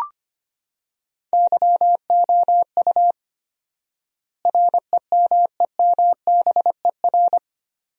Morse code, heard as a keyed tone, about 25 wpm, 700 Hz. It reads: YOU REMEMBER
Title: 4x Word-Spacing - Sentences from Top 400 Words - 25wpm